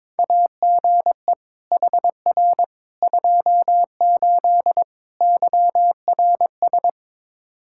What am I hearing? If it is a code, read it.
Morse code, heard as a keyed tone, about 22 wpm, 700 Hz. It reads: AGE HR 28 YRS
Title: Practice QSO - Exchange 3 - More Info - 22wpm